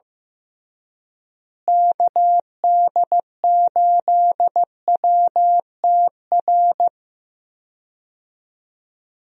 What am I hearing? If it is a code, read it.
Morse code, heard as a keyed tone, about 15 wpm, 700 Hz. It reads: KD8WTR
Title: USA Call Signs - 6 Characters - Repeated 3x - 15wpm